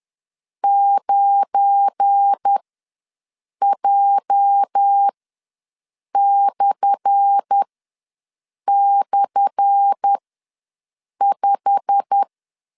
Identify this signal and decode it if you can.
Morse code, heard as a keyed tone, about 10 wpm, spaced as for 6 wpm, 800 Hz. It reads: 9J//5
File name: LEZIONI / 10 SIGNS / 10A SLASH ws 10wpm 4121